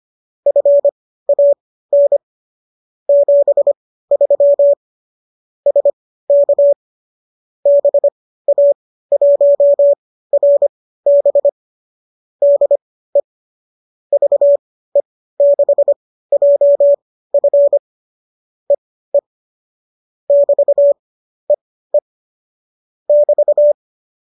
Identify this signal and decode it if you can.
Morse code, heard as a keyed tone, about 25 wpm, spaced as for 15 wpm, 570 Hz. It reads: FAN 73 SK BA1RB DE VE6JF EE <BT> EE <BT>
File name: qso203 15